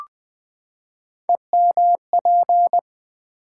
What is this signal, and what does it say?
Morse code, heard as a keyed tone, about 20 wpm, 700 Hz. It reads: EMP
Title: Build A Word - 3 Letters At A Time - 9-Letter Words - 20wpm